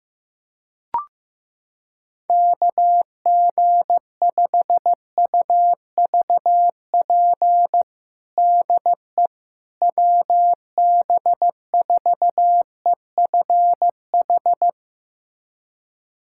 Morse code, heard as a keyed tone, about 15 wpm, 700 Hz. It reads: KG5UVP DE WB4EFH
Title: USA Callsign DE Callsign 15wpm